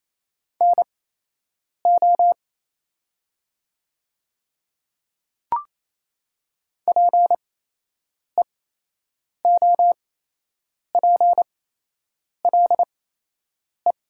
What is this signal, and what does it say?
Morse code, heard as a keyed tone, about 28 wpm, spaced as for 8 wpm, 700 Hz. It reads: NO PEOPLE
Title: 8x Character Spacing - Top 100 Words - 28wpm